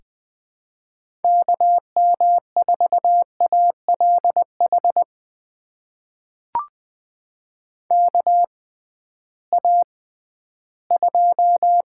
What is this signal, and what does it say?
Morse code, heard as a keyed tone, about 20 wpm, spaced as for 13 wpm, 700 Hz. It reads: KM4ALH KA2
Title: USA Call Signs - 6 Characters - 6x Character Spacing - 20wpm